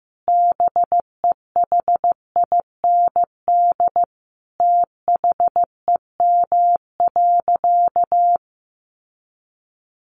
Morse code, heard as a keyed tone, about 14 wpm, 700 Hz. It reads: BEHIND THEM.
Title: Sentences from Top 1400 Words 15wpm